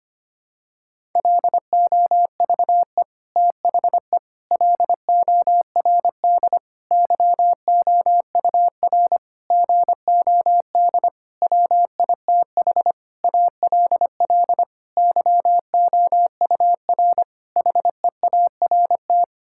Morse code, heard as a keyed tone, about 25 wpm, 700 Hz. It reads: LOVE THE LORD YOUR GOD WITH ALL YOUR HEART
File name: Sets of 9 Words - 25wpm